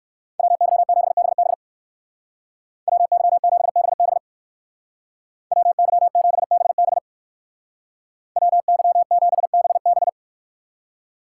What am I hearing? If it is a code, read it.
Morse code, heard as a keyed tone, about 64 wpm, 690 Hz. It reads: WY7BB WY7BB WY7BB WY7BB